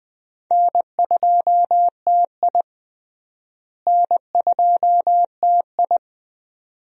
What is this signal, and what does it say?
Morse code, heard as a keyed tone, about 20 wpm, 700 Hz. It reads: N2TI N2TI
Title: USA Call Signs - 4 Characters - Repeated 2x - 20wpm